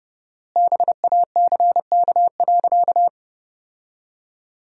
Morse code, heard as a keyed tone, about 32 wpm, 700 Hz. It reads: BACK.
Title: 4x Word-Spacing - Sentences from Top 200 Words - 30wpm